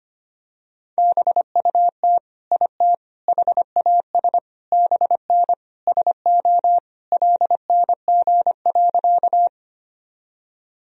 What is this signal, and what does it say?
Morse code, heard as a keyed tone, about 25 wpm, 700 Hz. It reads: BUT IT HAS BN SO LNG.
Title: Sentences from Top 100 Words and Abbreviations 25wpm